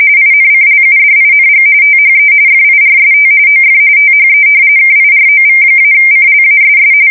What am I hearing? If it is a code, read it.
RTTY, about 45.45 baud. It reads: INCLUDING THE FIRST OF THE KIROV-CLASS
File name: fsk170 / battleshipnj